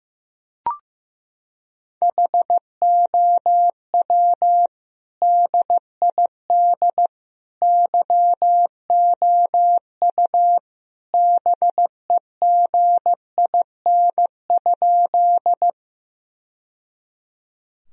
Morse code, heard as a keyed tone, about 16 wpm, 700 Hz. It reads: HOW DID YOU BEGIN?
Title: Sentences from Top 1000 Words - Review Part 1 - 15wpm